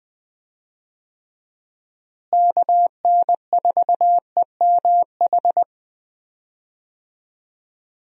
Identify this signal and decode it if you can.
Morse code, heard as a keyed tone, about 20 wpm, 700 Hz. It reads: KN4EMH
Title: USA Call Signs - 6 Characters - Repeated 3x - 20wpm